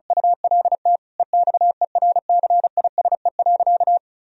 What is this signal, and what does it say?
Morse code, heard as a keyed tone, about 36 wpm, 700 Hz. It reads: ULT EXERCISE.
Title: Sentences from Top 1000 Words - Review Part 4 - 35wpm